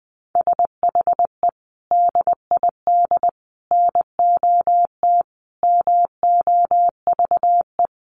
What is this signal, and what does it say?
Morse code, heard as a keyed tone, about 20 wpm, 700 Hz. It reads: SHE DID NOT MOVE
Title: Sentences from Top 1500 Words 20wpm